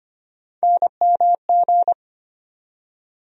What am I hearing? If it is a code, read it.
Morse code, heard as a keyed tone, about 25 wpm, 700 Hz. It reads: NMG
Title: USA Call Sign Postfixes - 3 Letters 25wpm